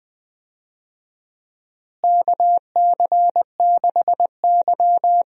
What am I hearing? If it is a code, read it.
Morse code, heard as a keyed tone, about 20 wpm, 700 Hz. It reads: KC6Y
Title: USA Call Signs - 4 Characters - Repeated 3x - 20wpm